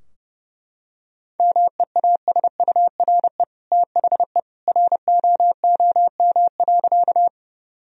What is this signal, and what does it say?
Morse code, heard as a keyed tone, about 30 wpm, 700 Hz. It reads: MEASURE THE ROOM.
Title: Sentences from Top 1000 Words - Review Part 2 - 30wpm